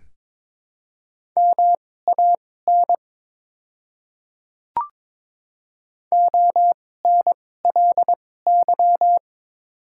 Morse code, heard as a keyed tone, about 22 wpm, spaced as for 16 wpm, 700 Hz. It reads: MAN ONLY
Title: 2x Character Spacing - Top 300 Words - 22wpm